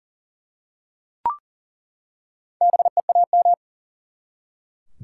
Morse code, heard as a keyed tone, about 40 wpm, 700 Hz. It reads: BEAM
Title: Top 90 QSO Elements 40wpm